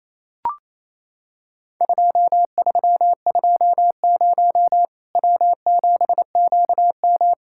Morse code, heard as a keyed tone, about 28 wpm, 700 Hz. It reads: 2320 W7QM
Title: Worked All Europe DX Contest - QTC Exchange - 28wpm